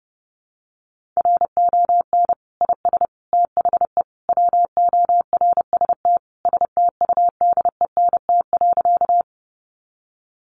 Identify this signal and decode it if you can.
Morse code, heard as a keyed tone, about 30 wpm, 700 Hz. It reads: RON IS THE WORST STUDENT.
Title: Additional Easy Phrases 30wpm